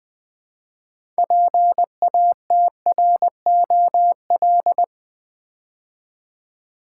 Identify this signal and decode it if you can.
Morse code, heard as a keyed tone, about 20 wpm, 700 Hz. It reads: PATROL
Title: Build A Word - 3 Letters At A Time - 6-Letter Words - 20wpm